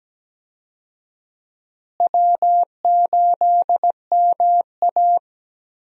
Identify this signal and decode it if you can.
Morse code, heard as a keyed tone, about 17 wpm, 700 Hz. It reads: W8MA
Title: USA Call Signs - 4 Characters - Repeated 2x - 17wpm